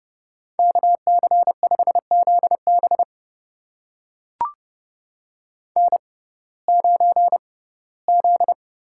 Morse code, heard as a keyed tone, about 31 wpm, spaced as for 19 wpm, 700 Hz. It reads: KC5ZB N9Z
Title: USA Call Signs - 5 Characters - 6x Character Spacing - 30wpm